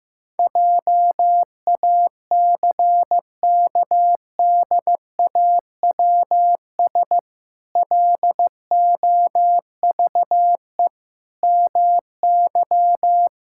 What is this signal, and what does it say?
Morse code, heard as a keyed tone, about 15 wpm, 700 Hz. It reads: JACKDAWS LOVE MY
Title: Pangram Challenge v2 - 15wpm